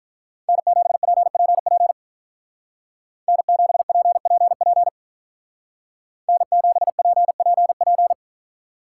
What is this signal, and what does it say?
Morse code, heard as a keyed tone, about 42 wpm, 700 Hz. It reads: N7PPP N7PPP N7PPP
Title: Speed Racing - USA 1x3 Call Signs 35wpm